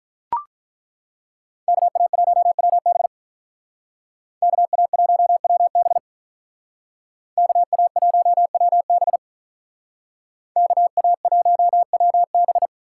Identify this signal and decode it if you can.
Morse code, heard as a keyed tone, about 42 wpm, 700 Hz. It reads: KA1WB KA1WB KA1WB KA1WB
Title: Speed Racing - USA 2x2 Call Signs 35wpm